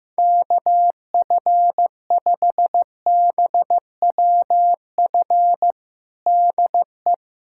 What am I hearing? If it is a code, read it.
Morse code, heard as a keyed tone, about 15 wpm, 700 Hz. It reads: KF5BWF DE
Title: USA Callsign DE Callsign - 6 Characters - 15wpm